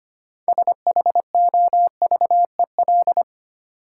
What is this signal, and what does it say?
Morse code, heard as a keyed tone, about 25 wpm, 700 Hz. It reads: SHOVEL
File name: practice / 25wpm / Easy Words with Hints - 25wpm